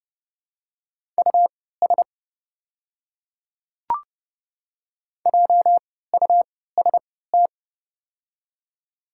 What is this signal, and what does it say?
Morse code, heard as a keyed tone, about 30 wpm, spaced as for 17 wpm, 700 Hz. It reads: US JUST
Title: 3x Character Spacing - Top 200 Words - 30wpm